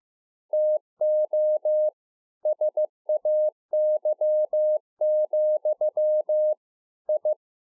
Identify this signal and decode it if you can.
Morse code, heard as a keyed tone, about 15 wpm, 610 Hz. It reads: TO SAY, I
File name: 15WPM